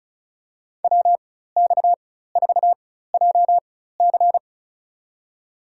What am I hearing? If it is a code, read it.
Morse code, heard as a keyed tone, about 36 wpm, spaced as for 16 wpm, 700 Hz. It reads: WX4JC
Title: USA Call Signs - 5 Characters - 4x Character Spacing - 35wpm